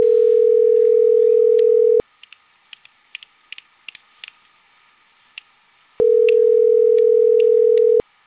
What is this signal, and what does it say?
Ringback tone (North America), 440 + 480 Hz.